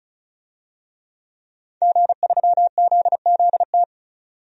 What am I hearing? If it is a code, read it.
Morse code, heard as a keyed tone, about 36 wpm, 700 Hz. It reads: G3ZZT
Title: DXCC-100 Call Signs - 5 Characters - 35wpm